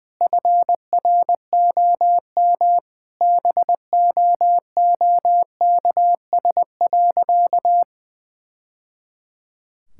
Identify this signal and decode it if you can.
Morse code, heard as a keyed tone, about 20 wpm, 700 Hz. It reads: FROM BOOKS.
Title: Sentences from Top 1000 Words - Review Part 6 - 20wpm